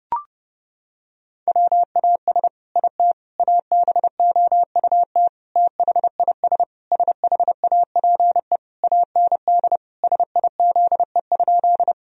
Morse code, heard as a keyed tone, about 30 wpm, 700 Hz. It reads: WAS IT ABOUT THIS SHAPE AND SIZE?
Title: Sentences from Top 1000 Words - Review Part 3 - 30wpm